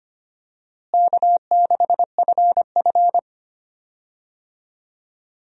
Morse code, heard as a keyed tone, about 25 wpm, 700 Hz. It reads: K6FF
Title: USA Call Signs - 4 Characters - Repeated 2x - 25wpm